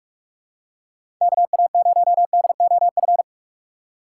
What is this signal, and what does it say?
Morse code, heard as a keyed tone, about 46 wpm, 700 Hz. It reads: K A 0 D O F
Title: USA Call Signs - 6 Characters - 6x Character Spacing - 45wpm